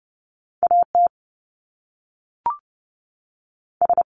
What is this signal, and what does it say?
Morse code, heard as a keyed tone, about 30 wpm, 700 Hz. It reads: AT S
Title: Top Two Letter Combinations in Words 30wpm